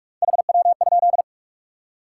Morse code, heard as a keyed tone, about 46 wpm, 700 Hz. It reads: HW?
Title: Top 90 QSO Elements 45wpm